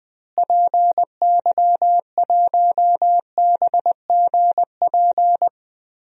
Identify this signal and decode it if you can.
Morse code, heard as a keyed tone, about 20 wpm, 700 Hz. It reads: PY1BGP
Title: DXCC-100 Call Signs - 6 Characters - Repeated 3x - 20wpm